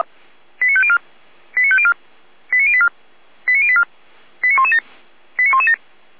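Five-tone selective call (ZVEI), heard as quick runs of five tones, tones of about 70 ms. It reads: 89504 89504 89084 89084 89117 89117
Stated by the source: ZVEI five-tone selective call received on SDR.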